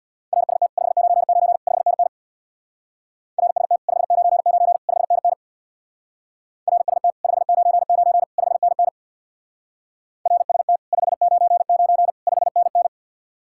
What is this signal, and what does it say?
Morse code, heard as a keyed tone, about 66 wpm, 700 Hz. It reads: RST 599 5NN RST 599 5NN RST 599 5NN RST 599 5NN